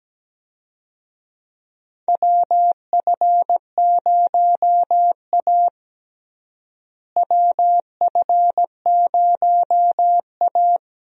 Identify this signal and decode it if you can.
Morse code, heard as a keyed tone, about 17 wpm, 700 Hz. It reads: WF0A WF0A
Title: USA Call Signs - 4 Characters - Repeated 2x - 17wpm